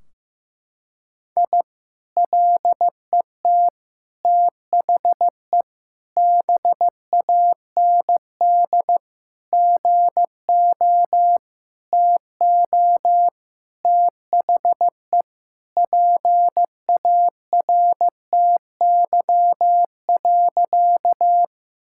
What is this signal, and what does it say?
Morse code, heard as a keyed tone, about 15 wpm, 700 Hz. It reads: I LET THE BAND GO TO THE PARTY.
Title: Sentences from Top 1600 Words 15wpm